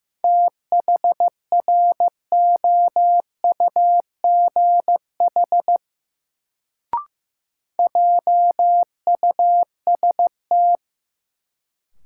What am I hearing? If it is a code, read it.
Morse code, heard as a keyed tone, about 15 wpm, 700 Hz. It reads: THROUGH JUST
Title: Sentences from Top 200 Words - New Word Drill - 15wpm